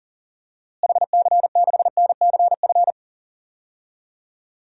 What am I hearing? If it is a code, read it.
Morse code, heard as a keyed tone, about 40 wpm, 700 Hz. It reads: HC6NCF
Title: DXCC-100 Call Signs - 6 Characters - Repeated 3x - 40wpm